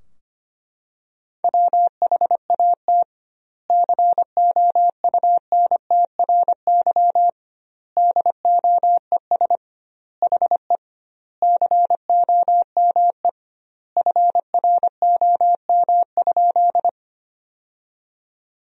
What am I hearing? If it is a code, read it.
Morse code, heard as a keyed tone, about 25 wpm, 700 Hz. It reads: WHAT COUNTRY DOES HE COME FROM?
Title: 2x Word-Spacing - Sentences from Top 300 Words - 25wpm